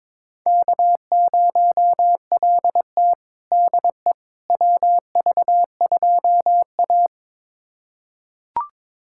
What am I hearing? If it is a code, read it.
Morse code, heard as a keyed tone, about 22 wpm, 700 Hz. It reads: K0LT DE WV2A
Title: USA Callsign DE Callsign - 4 Characters - 22wpm